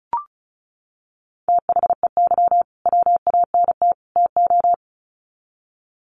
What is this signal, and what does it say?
Morse code, heard as a keyed tone, about 34 wpm, 700 Hz. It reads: THEY WANT TO
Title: Sets of 3 Words 35wpm